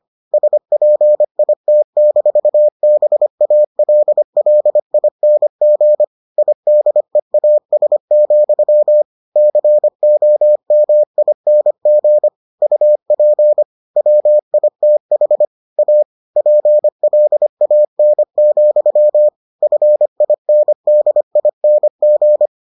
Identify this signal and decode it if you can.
Morse code, heard as a keyed tone, about 25 wpm, 600 Hz. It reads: SPIT-BALLING IDEAS, COMING UP WITH A PLAN, FINDING